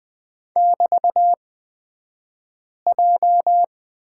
Morse code, heard as a keyed tone, about 20 wpm, spaced as for 5 wpm, 700 Hz. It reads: <BT>J
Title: 249 Lesson 40d - BK - Two Characters 20wpm